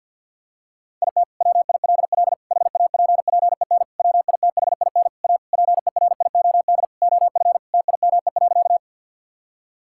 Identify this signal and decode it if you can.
Morse code, heard as a keyed tone, about 46 wpm, 690 Hz. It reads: IT WILL HAPPEN WITHIN A PERIOD OF TIME.
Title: Sentences from Top 1000 Words - Review Part 5 - 50wpm